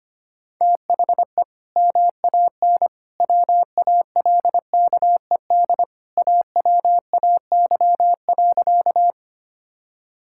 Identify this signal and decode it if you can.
Morse code, heard as a keyed tone, about 25 wpm, 690 Hz. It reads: THE MAN WALKED AWAY.
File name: practice / 25wpm / Sentences with 2-Word Phrases - 25wpm